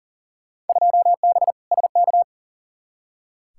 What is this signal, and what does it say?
Morse code, heard as a keyed tone, about 40 wpm, 700 Hz. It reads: 2B SK
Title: Field Day Exchanges Encore - 40wpm